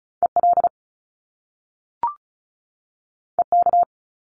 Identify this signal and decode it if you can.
Morse code, heard as a keyed tone, about 36 wpm, 700 Hz. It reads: EL EK
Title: Top Two Letter Combinations in Words 35wpm